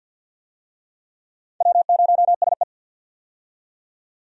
Morse code, heard as a keyed tone, about 54 wpm, 700 Hz. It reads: W0SE
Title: USA Call Signs - 4 Characters - 50wpm